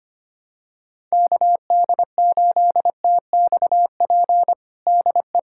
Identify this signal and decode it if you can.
Morse code, heard as a keyed tone, about 25 wpm, 700 Hz. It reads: KD8TXP DE
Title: USA Callsign DE Callsign - 6 Characters - 25wpm